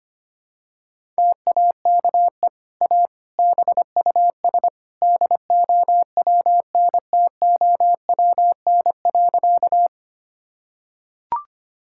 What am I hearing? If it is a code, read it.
Morse code, heard as a keyed tone, about 25 wpm, 700 Hz. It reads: TAKE A BUS DOWNTOWN.
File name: Easy Sentences with Hints - 25wpm